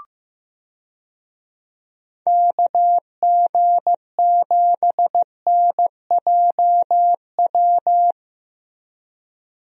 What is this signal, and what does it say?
Morse code, heard as a keyed tone, about 15 wpm, 700 Hz. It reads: KG7NJW
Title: USA Call Signs - 6 Characters - 15wpm